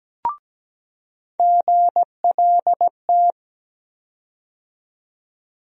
Morse code, heard as a keyed tone, about 17 wpm, 700 Hz. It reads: GLT
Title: USA Call Sign Postfixes - 3 Letters 17wpm